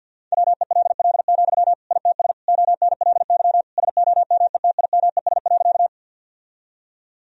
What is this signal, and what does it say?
Morse code, heard as a keyed tone, about 50 wpm, 700 Hz. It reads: WELL, ITS ONLY SOMETIMES.
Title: Sentences from Top 1000 Words - Review Part 3 - 50wpm